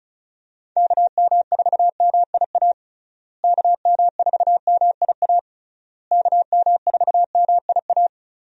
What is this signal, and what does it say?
Morse code, heard as a keyed tone, about 34 wpm, 700 Hz. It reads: KM4MIA KM4MIA KM4MIA
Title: USA Call Signs - 6 Characters - Repeated 3x - 35wpm